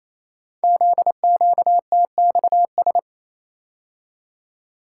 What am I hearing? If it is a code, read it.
Morse code, heard as a keyed tone, about 28 wpm, 700 Hz. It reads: ZQTXS
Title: Code Groups - 5 Letters - 28wpm